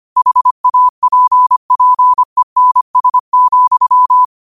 Morse code, heard as a keyed tone, about 25 wpm, 1000 Hz. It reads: HAPPENS,